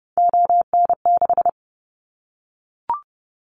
Morse code, heard as a keyed tone, about 30 wpm, 700 Hz. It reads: ON6
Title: Callsign Prefixes for Top 100 DXCC - 30wpm